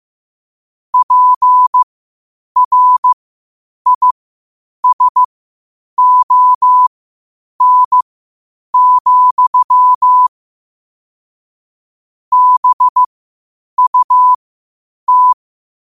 Morse code, heard as a keyed tone, about 15 wpm, spaced as for 8 wpm, 1000 Hz. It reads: PRISON, BUT
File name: qotdfiles / 8wpm